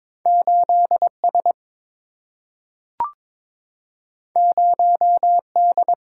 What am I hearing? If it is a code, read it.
Morse code, heard as a keyed tone, about 22 wpm, 700 Hz. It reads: 8S 0D
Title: Single-Letter Single-Number 22wpm